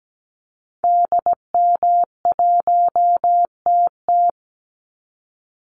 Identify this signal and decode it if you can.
Morse code, heard as a keyed tone, about 17 wpm, 700 Hz. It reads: DM1TT
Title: Call Signs Repeated 3x - 17wpm